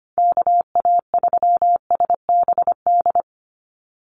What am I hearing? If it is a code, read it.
Morse code, heard as a keyed tone, about 25 wpm, 700 Hz. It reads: KA3SBD
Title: USA Call Signs 25wpm